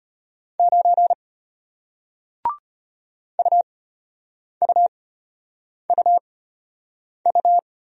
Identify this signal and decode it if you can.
Morse code, heard as a keyed tone, about 33 wpm, spaced as for 8 wpm, 700 Hz. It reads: 9 UUUU